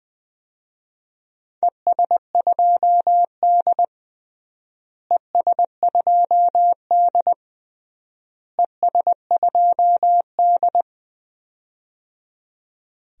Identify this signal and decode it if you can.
Morse code, heard as a keyed tone, about 21 wpm, 700 Hz. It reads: ES2D ES2D ES2D
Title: DXCC-100 Call Signs - 4 Characters - Repeated 3x - 20wpm